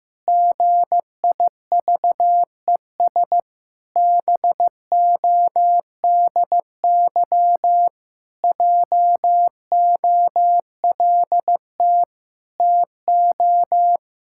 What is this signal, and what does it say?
Morse code, heard as a keyed tone, about 15 wpm, 700 Hz. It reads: GIVES BODY JOLT TO
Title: Pangram Challenge v2 - 15wpm